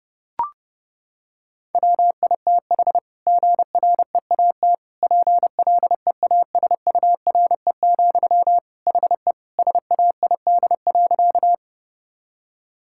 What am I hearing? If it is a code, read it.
Morse code, heard as a keyed tone, about 32 wpm, 700 Hz. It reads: WITH GREAT PLEASURE, HE SAID.
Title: Sentences from Top 1600 Words 30wpm